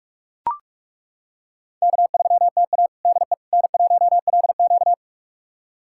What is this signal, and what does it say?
Morse code, heard as a keyed tone, about 41 wpm, 700 Hz. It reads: K3TA DE N1LQ
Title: USA Callsign DE Callsign - 4 Characters - 45wpm